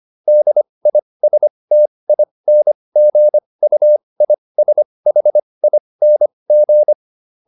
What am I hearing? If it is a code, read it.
Morse code, heard as a keyed tone, about 25 wpm, spaced as for 20 wpm, 600 Hz. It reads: DISTINGUISHING
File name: SS316 20